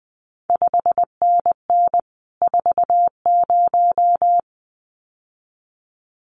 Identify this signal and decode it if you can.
Morse code, heard as a keyed tone, about 20 wpm, 700 Hz. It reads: 5NN 40
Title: CQ WW DX Contest Exchange Encore - 20wpm